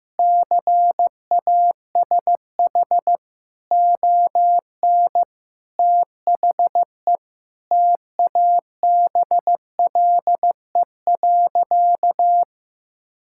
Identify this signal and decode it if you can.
Morse code, heard as a keyed tone, about 15 wpm, 700 Hz. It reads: CASH ON THE TABLE.
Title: Sentences from Top 1000 Words - Review Part 5 - 15wpm